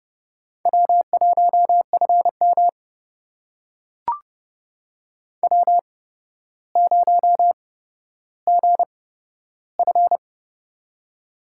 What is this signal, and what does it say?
Morse code, heard as a keyed tone, about 32 wpm, spaced as for 13 wpm, 700 Hz. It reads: W1FM W0GF
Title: USA Call Signs - 4 Characters - 8x Character Spacing - 30wpm